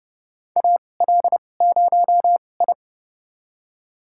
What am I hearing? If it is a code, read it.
Morse code, heard as a keyed tone, about 30 wpm, spaced as for 22 wpm, 700 Hz. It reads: AL0I
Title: USA Call Signs - 4 Characters - 2x Character Spacing - 30wpm